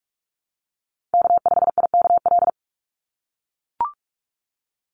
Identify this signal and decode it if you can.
Morse code, heard as a keyed tone, about 44 wpm, 700 Hz. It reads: K5IKL
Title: USA Call Signs 45wpm